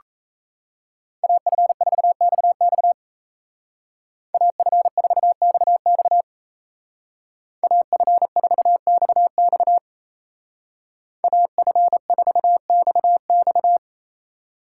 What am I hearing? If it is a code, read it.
Morse code, heard as a keyed tone, about 34 wpm, 700 Hz. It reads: AF4XX AF4XX AF4XX AF4XX
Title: Speed Racing - USA 2x2 Call Signs 28wpm